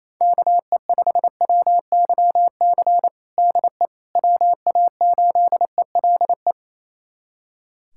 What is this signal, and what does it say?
Morse code, heard as a keyed tone, about 28 wpm, 700 Hz. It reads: KE5WYC DE WA8ELE
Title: USA Callsign DE Callsign - 6 Characters - 28wpm